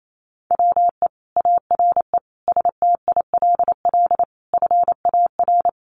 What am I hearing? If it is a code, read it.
Morse code, heard as a keyed tone, about 28 wpm, 700 Hz. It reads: WE ARE STILL FAR
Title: Sentences from Top 1300 Words 28wpm